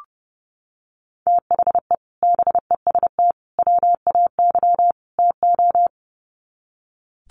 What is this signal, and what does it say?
Morse code, heard as a keyed tone, about 30 wpm, 700 Hz. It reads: THE BEST WAY TO
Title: Sets of 4 Words 30wpm